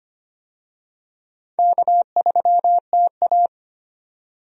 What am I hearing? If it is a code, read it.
Morse code, heard as a keyed tone, about 25 wpm, 700 Hz. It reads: K3TA
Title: USA Call Signs - 4 Characters - 25wpm